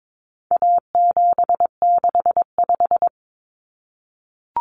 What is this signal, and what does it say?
Morse code, heard as a keyed tone, about 22 wpm, 700 Hz. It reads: A765
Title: Common Cut Numbers - 4-Digit 22wpm